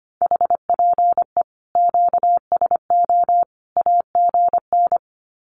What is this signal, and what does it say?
Morse code, heard as a keyed tone, about 25 wpm, 700 Hz. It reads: HPE QSO AGN
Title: Common QSO Phrases 25wpm